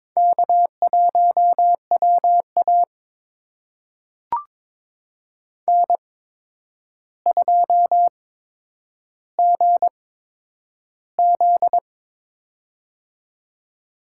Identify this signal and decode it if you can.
Morse code, heard as a keyed tone, about 23 wpm, spaced as for 9 wpm, 700 Hz. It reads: K1WA N2GZ
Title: USA Call Signs - 4 Characters - 8x Character Spacing - 22wpm